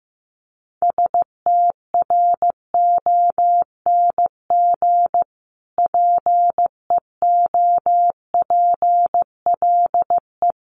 Morse code, heard as a keyed tone, about 15 wpm, 700 Hz. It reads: STRONG PEOPLE
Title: Sentences from Top 1500 Words 15wpm